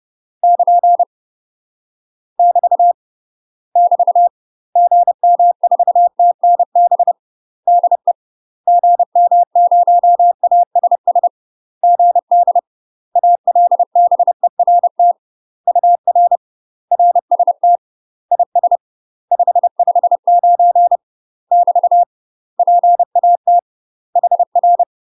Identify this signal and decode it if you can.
Morse code, heard as a keyed tone, about 30 wpm, 700 Hz. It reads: <KN> <BT> <BT> GM4TNB DE GM0ASS GD ALBERT UR RST IS 559 <BT> PAT HR
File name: UK MASTER QSOs TRAINING/30WPM